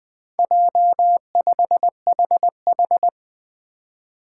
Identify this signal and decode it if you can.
Morse code, heard as a keyed tone, about 20 wpm, 700 Hz. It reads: J5HH
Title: Four Challenging Letter-Number Combinations 20wpm